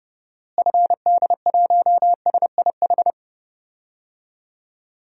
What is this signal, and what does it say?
Morse code, heard as a keyed tone, about 32 wpm, 700 Hz. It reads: FD1SIH
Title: DXCC-100 Call Signs - 6 Characters - Repeated 3x - 30wpm